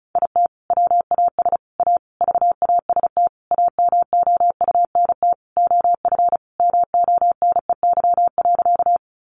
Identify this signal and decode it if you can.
Morse code, heard as a keyed tone, about 34 wpm, 700 Hz. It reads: IT WAS A VAST AMOUNT OF MONEY.
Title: Sentences from Top 1300 Words 35wpm